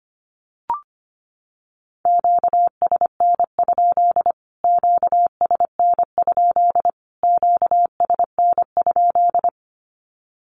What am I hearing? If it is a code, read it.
Morse code, heard as a keyed tone, about 25 wpm, 700 Hz. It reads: QSN? QSN? QSN?
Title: Q-Codes Repeated 3x 25wpm